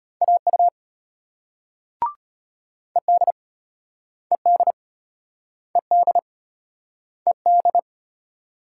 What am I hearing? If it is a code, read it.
Morse code, heard as a keyed tone, about 32 wpm, 700 Hz. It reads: AU ED ED ED ED